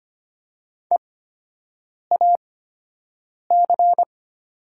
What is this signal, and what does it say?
Morse code, heard as a keyed tone, about 25 wpm, spaced as for 7 wpm, 700 Hz. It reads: EAC
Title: 8x Character Spacing - Top 100 Words - 25wpm